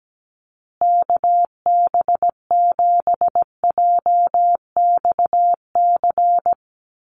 Morse code, heard as a keyed tone, about 17 wpm, 700 Hz. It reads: KB7JXC
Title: USA Call Signs Repeated 3x 17wpm